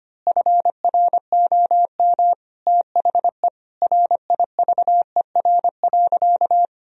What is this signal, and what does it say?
Morse code, heard as a keyed tone, about 25 wpm, 700 Hz. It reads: FROM THE RIVER.
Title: Sentences from Top 1000 Words - Review Part 1 - 25wpm